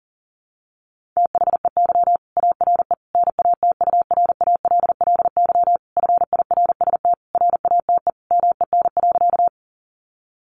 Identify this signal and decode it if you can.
Morse code, heard as a keyed tone, about 40 wpm, 700 Hz. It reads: THEY ARE NATURALLY FIRST RATE MEN.
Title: Sentences from Top 1300 Words 40wpm